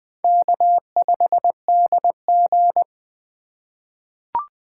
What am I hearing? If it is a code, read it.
Morse code, heard as a keyed tone, about 20 wpm, 700 Hz. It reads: K5DG
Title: USA Call Signs - 4 Characters - 3x Character Spacing - 20wpm